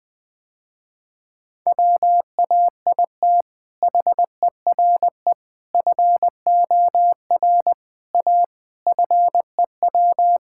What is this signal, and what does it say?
Morse code, heard as a keyed tone, about 20 wpm, 700 Hz. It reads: WAIT HERE FOR A FEW
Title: Sentences with 3-Word Phrases - Spoken Beginning - 20wpm